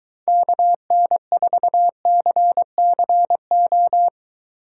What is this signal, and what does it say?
Morse code, heard as a keyed tone, about 23 wpm, 700 Hz. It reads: KN4CCO